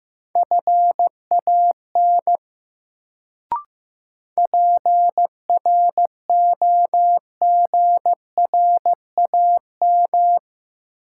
Morse code, heard as a keyed tone, about 16 wpm, 700 Hz. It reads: FAN PROGRAM
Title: Sentences from Top 1000 Words - Review Part 4 - New Word Drill - 15wpm